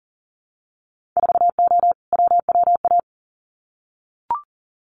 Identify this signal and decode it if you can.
Morse code, heard as a keyed tone, about 39 wpm, 700 Hz. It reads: 4O WWA
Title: Winter Field Day Exchanges 40wpm